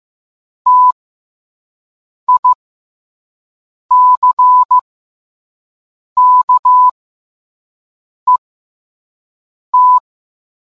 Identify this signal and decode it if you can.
Morse code, heard as a keyed tone, about 15 wpm, spaced as for 5 wpm, 1000 Hz. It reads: TICKET